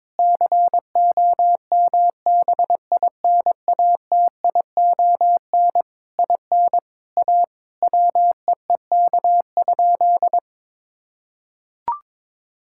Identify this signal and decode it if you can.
Morse code, heard as a keyed tone, about 22 wpm, 700 Hz. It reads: COMBINATION IN A WEEK?
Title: Sentences from Top 1000 Words - Review Part 6 - 22wpm